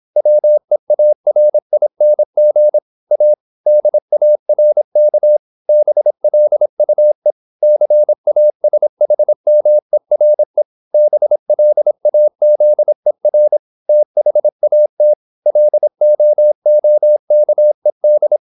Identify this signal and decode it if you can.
Morse code, heard as a keyed tone, about 26 wpm, 600 Hz. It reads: WEARING A DARK BLUE CASHMERE BLAZER THAT LOOKED
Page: Never 26wpm | CW med Gnister